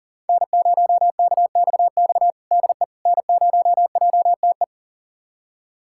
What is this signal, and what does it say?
Morse code, heard as a keyed tone, about 40 wpm, 700 Hz. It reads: N0KXX DE N0JTE
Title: USA Callsign DE Callsign - 5 Characters - 40wpm